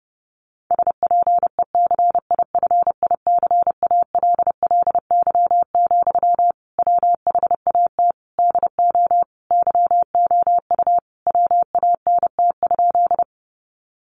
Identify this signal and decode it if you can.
Morse code, heard as a keyed tone, about 30 wpm, 700 Hz. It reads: SPECIFICALLY, WHAT DO YOU WANT?
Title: Sentences from Top 1100 Words 30wpm